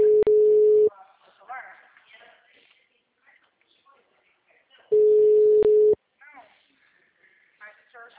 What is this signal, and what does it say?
Ringback tone (Europe, 425 Hz), with speech.
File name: out- -2018- - - .25197